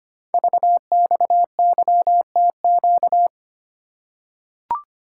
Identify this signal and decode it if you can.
Morse code, heard as a keyed tone, about 25 wpm, 700 Hz. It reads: VXYTQ
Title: Code Groups - 5 Letter:Numbers - 25wpm